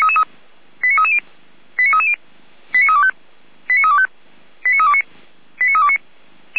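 Five-tone selective call (ZVEI), heard as quick runs of five tones, tones of about 70 ms. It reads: 89330 89330 89325 89325 89329 89329
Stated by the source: ZVEI five-tone selective call 89509, received on an SDR.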